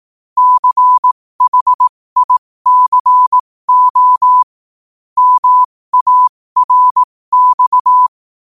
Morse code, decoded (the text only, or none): CHICO MARX